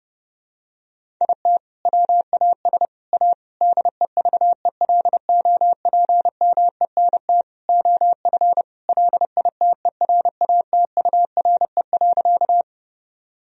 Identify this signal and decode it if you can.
Morse code, heard as a keyed tone, about 30 wpm, 700 Hz. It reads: IT WAS A DEVELOPMENT OF LITERATURE.
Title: Sentences from Top 1000 Words - Review Part 4 - 30wpm